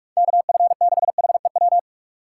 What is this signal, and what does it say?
Morse code, heard as a keyed tone, about 46 wpm, 700 Hz. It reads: KF6HEW
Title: USA Call Signs - 6x Character Spacing - 45wpm